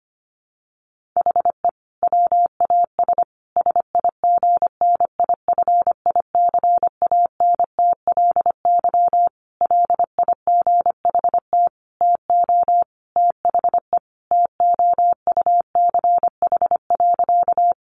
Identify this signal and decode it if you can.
Morse code, heard as a keyed tone, about 25 wpm, 700 Hz. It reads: HE WAS SIGNIFICANTLY LIGHT TO THE TOUCH.
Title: Sentences from Top 1500 Words 25wpm